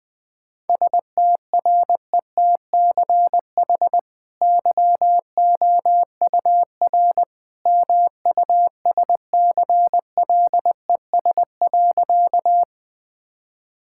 Morse code, heard as a keyed tone, about 20 wpm, 700 Hz. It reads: STRETCH YOUR MUSCLES.
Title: Easy Sentences with Hints - 20wpm